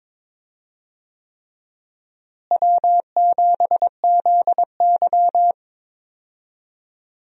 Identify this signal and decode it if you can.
Morse code, heard as a keyed tone, about 22 wpm, 700 Hz. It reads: W7ZY
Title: USA Call Signs - 4 Characters - Repeated 3x - 22wpm